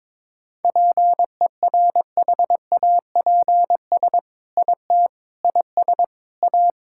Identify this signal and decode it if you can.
Morse code, heard as a keyed tone, about 22 wpm, 700 Hz. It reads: PERHAPS IT IS A
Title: Sentences from Top 1000 Words - Review Part 5 - 22wpm